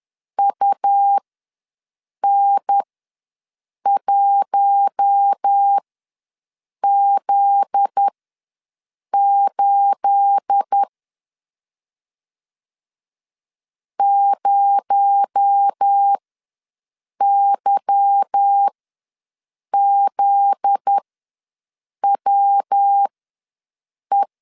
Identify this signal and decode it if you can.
Morse code, heard as a keyed tone, about 10 wpm, spaced as for 6 wpm, 800 Hz. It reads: UN1Z8 0YZWE
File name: LEZIONI / 9 2867 ws 10wpm 4215